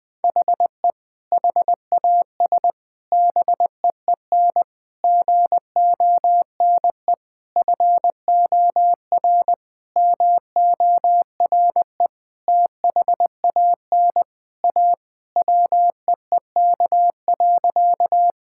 Morse code, heard as a keyed tone, about 20 wpm, 700 Hz. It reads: HE HAS BEEN GONE FOR MORE THAN A WEEK.
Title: Sentences with 4-Word Phrases - 20wpm